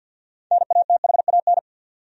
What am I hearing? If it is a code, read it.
Morse code, heard as a keyed tone, about 47 wpm, 700 Hz. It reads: NATHAN
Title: USA Common Names - 50wpm